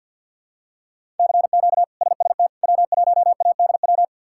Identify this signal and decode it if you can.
Morse code, heard as a keyed tone, about 52 wpm, 700 Hz. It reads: CQ SST W1ADW
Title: K1USN SST QSO Practice - 50wpm